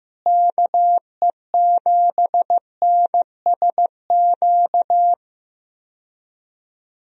Morse code, heard as a keyed tone, about 15 wpm, 700 Hz. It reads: KE7NSQ
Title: USA Call Signs - 6 Characters - Repeated 3x - 15wpm